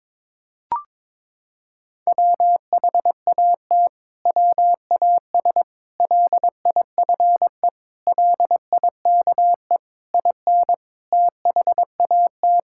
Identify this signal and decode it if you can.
Morse code, heard as a keyed tone, about 22 wpm, 700 Hz. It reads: WHAT WAS LIFE LIKE IN THAT
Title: Sentences from Top 1000 Words - Review Part 5 - 22wpm